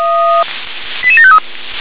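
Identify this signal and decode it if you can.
Five-tone selective call (ZVEI), heard as quick runs of five tones, tones of about 70 ms. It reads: 88753
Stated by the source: ZVEI five-tone call received by SDR.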